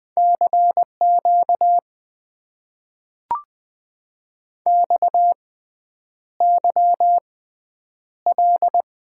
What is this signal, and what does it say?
Morse code, heard as a keyed tone, about 20 wpm, spaced as for 9 wpm, 700 Hz. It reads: CQ XYL
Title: Top 90 QSO Elements - 6x Character Spacing - 20wpm